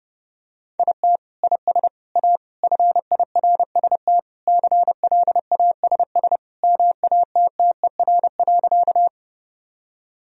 Morse code, heard as a keyed tone, about 32 wpm, 700 Hz. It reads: IT IS A FIRST CLASS MATTER.
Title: Sentences from Top 1000 Words - Review Part 2 - 30wpm